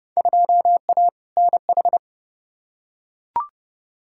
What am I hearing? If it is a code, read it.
Morse code, heard as a keyed tone, about 32 wpm, 700 Hz. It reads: 2A NH
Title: Field Day Exchanges - 30wpm